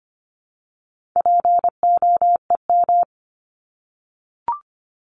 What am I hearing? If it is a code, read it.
Morse code, heard as a keyed tone, about 25 wpm, 700 Hz. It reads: POEM
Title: Top 1751-2000 Words 25wpm